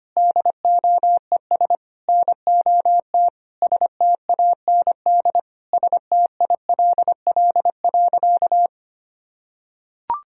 Morse code, heard as a keyed tone, about 25 wpm, 700 Hz. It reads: DOES NOT STAND STILL.
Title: Sentences from Top 1000 Words - Review Part 1 - 25wpm